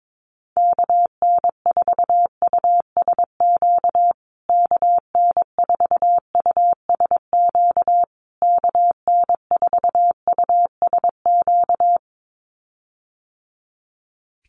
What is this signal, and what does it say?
Morse code, heard as a keyed tone, about 22 wpm, 700 Hz. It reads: KN4USQ KN4USQ KN4USQ
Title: USA Call Signs Repeated 3x 22wpm